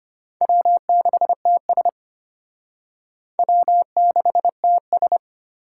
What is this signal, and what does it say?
Morse code, heard as a keyed tone, about 34 wpm, 700 Hz. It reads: W6TS W6TS